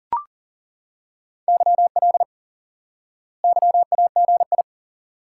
Morse code, heard as a keyed tone, about 37 wpm, 700 Hz. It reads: YL YAGI
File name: IQE Course 3 - Lesson 89 - YAGI - Two Elements - 40wpm